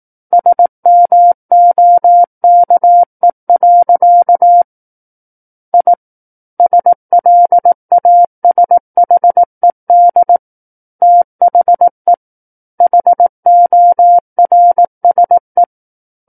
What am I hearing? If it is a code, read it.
Morse code, heard as a keyed tone, about 18 wpm, 700 Hz. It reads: SMOKE. I SLASHED THE HORSE
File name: Book One / 09-Chapter 9 - 18 WPM